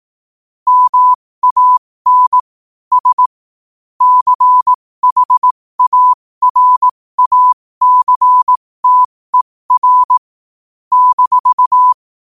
Morse code, heard as a keyed tone, about 18 wpm, spaced as for 15 wpm, 1000 Hz. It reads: MAN S CHARACTER -
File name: qotdfiles / 15wpm